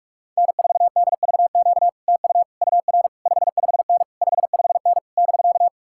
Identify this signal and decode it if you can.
Morse code, heard as a keyed tone, about 48 wpm, 700 Hz. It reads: N4DVQ TU UR 55N 55N <BK>